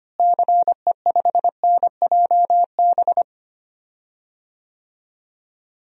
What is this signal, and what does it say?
Morse code, heard as a keyed tone, about 25 wpm, 700 Hz. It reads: CE5NJB
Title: DXCC-100 Call Signs - 6 Characters - Repeated 3x - 25wpm